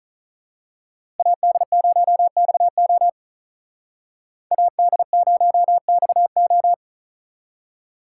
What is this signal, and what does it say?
Morse code, heard as a keyed tone, about 39 wpm, 700 Hz. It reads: AD0XO AD0XO
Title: Speed Racing - USA 2x2 Call Signs 35wpm